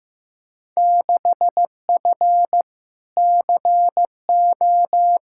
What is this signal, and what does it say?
Morse code, heard as a keyed tone, about 15 wpm, 700 Hz. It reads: 6F CO
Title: Field Day Exchanges Encore - 15wpm